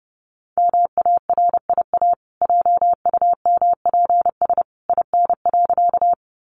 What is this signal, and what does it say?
Morse code, heard as a keyed tone, about 28 wpm, 700 Hz. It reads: MARIA JUMPS IN.
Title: Easy Phrases 30wpm